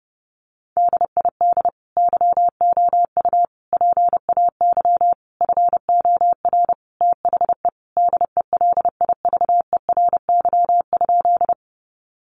Morse code, heard as a keyed tone, about 30 wpm, 700 Hz. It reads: DID YOU PAY FOR THE DELIVERY?
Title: Sentences from Top 1500 Words 30wpm